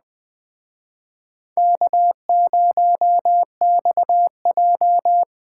Morse code, heard as a keyed tone, about 21 wpm, 700 Hz. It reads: K0XJ
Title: Speed Racing - USA 1x2 Call Signs 17wpm